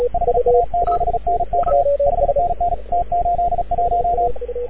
Morse code, heard as a keyed tone, about 36 wpm, 690 Hz, with radio noise. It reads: HA7NK 5NN T81
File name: AUDIO2019 / HG6N CW SSB MIX